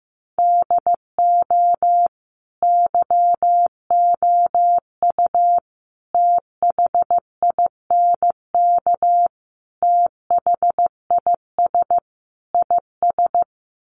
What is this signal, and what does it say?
Morse code, heard as a keyed tone, about 15 wpm, 700 Hz. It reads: DO YOU THINK THIS IS
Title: Sets of 5 Words Encore2 15wpm